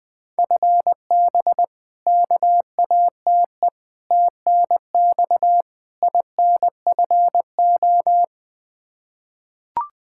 Morse code, heard as a keyed tone, about 20 wpm, 700 Hz. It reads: FB KATE TNX INFO
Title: Practice QSO - Exchange 3 - More Info - 20wpm